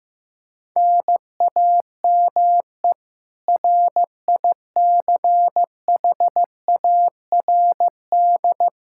Morse code, heard as a keyed tone, about 15 wpm, 700 Hz. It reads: NAME RICHARD
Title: Practice QSO - Exchange 1 - Beginning - 15wpm